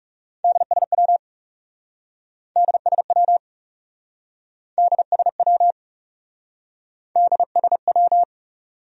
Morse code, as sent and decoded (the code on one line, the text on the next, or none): -.. ... .-- / -.. ... .-- / -.. ... .-- / -.. ... .--
DSW DSW DSW DSW